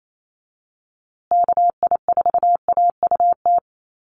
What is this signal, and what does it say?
Morse code, heard as a keyed tone, about 28 wpm, 700 Hz. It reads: KI4AUT
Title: USA Call Signs Repeated 2x 28wpm